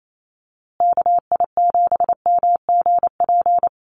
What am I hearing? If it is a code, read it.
Morse code, heard as a keyed tone, about 28 wpm, 700 Hz. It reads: KI7MGP
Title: USA Call Signs Repeated 2x 28wpm